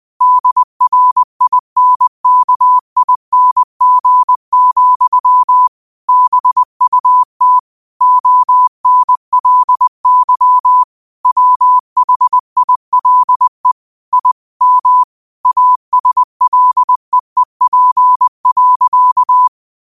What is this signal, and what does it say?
Morse code, heard as a keyed tone, about 20 wpm, 1000 Hz. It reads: DRINKING, BUT ONLY WHILE I M ASLEEP.
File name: qotdfiles / 20wpm